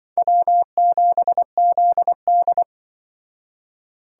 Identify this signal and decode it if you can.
Morse code, heard as a keyed tone, about 24 wpm, 700 Hz. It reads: W7ZD
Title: Speed Racing - USA Call Signs 20wpm